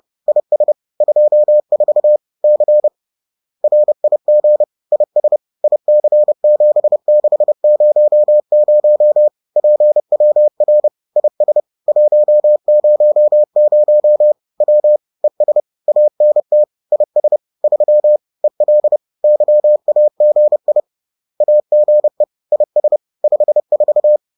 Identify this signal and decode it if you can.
Morse code, heard as a keyed tone, about 30 wpm, 600 Hz. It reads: IS 24 C RIG IS IC7600 PWR IS 100 W ES ANT IS 3 EL YAGI AGE IS 54